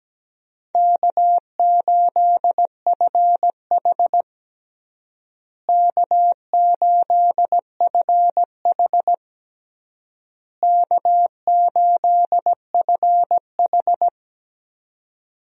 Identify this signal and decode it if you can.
Morse code, heard as a keyed tone, about 17 wpm, 700 Hz. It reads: K8FH K8FH K8FH
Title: USA Call Signs - 4 Characters - Repeated 3x - 17wpm